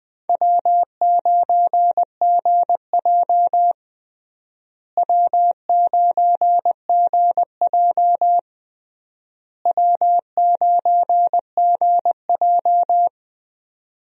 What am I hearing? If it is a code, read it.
Morse code, heard as a keyed tone, about 21 wpm, 700 Hz. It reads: W9GJ W9GJ W9GJ
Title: USA Call Signs - 4 Characters - Repeated 3x - 20wpm